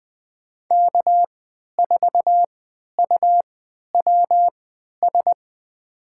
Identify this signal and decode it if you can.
Morse code, heard as a keyed tone, about 20 wpm, spaced as for 11 wpm, 700 Hz. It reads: K4UWS
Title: USA Call Signs - 5 Characters - 3x Character Spacing - 20wpm